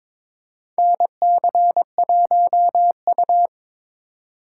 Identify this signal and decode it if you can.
Morse code, heard as a keyed tone, about 22 wpm, 700 Hz. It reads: NC1U
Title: USA Call Signs - 4 Characters - 2x Character Spacing - 22wpm